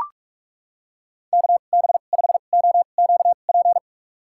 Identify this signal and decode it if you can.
Morse code, heard as a keyed tone, about 43 wpm, spaced as for 32 wpm, 700 Hz. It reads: KB5OQP
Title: USA Call Signs - 6 Characters - 2x Character Spacing - 45wpm